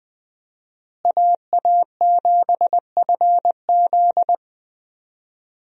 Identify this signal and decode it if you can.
Morse code, heard as a keyed tone, about 20 wpm, 700 Hz. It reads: AA7FZ
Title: USA Call Signs - 5 Characters - Repeated 2x - 20wpm